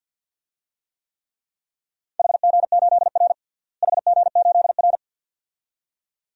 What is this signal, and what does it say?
Morse code, heard as a keyed tone, about 53 wpm, 700 Hz. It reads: HG8R HG8R
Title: DXCC-100 Call Signs - 4 Characters - Repeated 2x - 50wpm